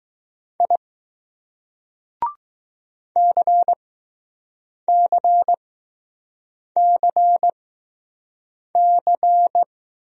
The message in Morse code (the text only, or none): I CCCC